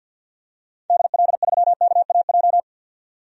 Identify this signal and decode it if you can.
Morse code, heard as a keyed tone, about 52 wpm, 700 Hz. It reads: DL3KAJ
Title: DXCC-100 Call Signs - 6 Characters - 50wpm